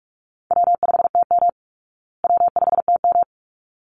Morse code, heard as a keyed tone, about 44 wpm, 700 Hz. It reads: W5TM W5TM
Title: Easy Call Signs Repeated 2x - 45wpm